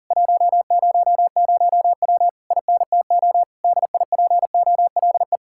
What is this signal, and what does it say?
Morse code, heard as a keyed tone, about 40 wpm, 700 Hz. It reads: 100W INTO DIPOLE